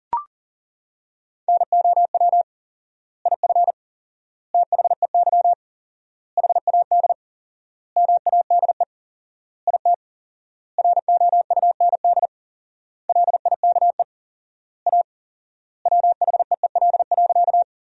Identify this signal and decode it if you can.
Morse code, heard as a keyed tone, about 37 wpm, 700 Hz. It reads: NOW IF THEY HAD MADE IT ROUND LIKE A WHEEL.
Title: 4x Word-Spacing - Sentences from Top 500 Words - 40wpm